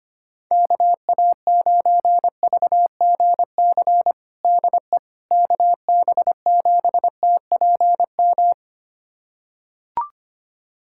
Morse code, heard as a keyed tone, about 25 wpm, 700 Hz. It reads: KA9VGC DE KB7TPM
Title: USA Callsign DE Callsign - 6 Characters - 25wpm